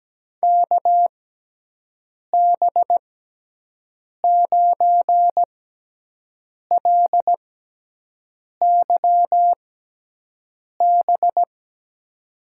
Morse code, heard as a keyed tone, about 17 wpm, 700 Hz. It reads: K B 9 L Y B
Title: USA Call Signs - 6 Characters - 6x Character Spacing - 17wpm